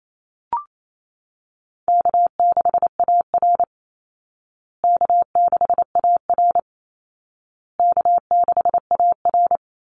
Morse code, heard as a keyed tone, about 28 wpm, 700 Hz. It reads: K6AR K6AR K6AR
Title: Call Signs Repeated 2x - 28wpm